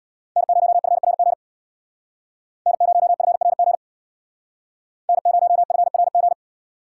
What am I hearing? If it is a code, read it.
Morse code, heard as a keyed tone, about 59 wpm, 700 Hz. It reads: N0FRG N0FRG N0FRG
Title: Speed Racing - USA 1x3 Call Signs 50wpm